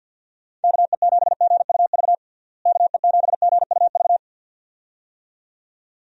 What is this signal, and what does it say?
Morse code, heard as a keyed tone, about 51 wpm, 700 Hz. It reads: KE7GUV KE7GUV
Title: USA Call Signs - 6 Characters - Repeated 2x - 50wpm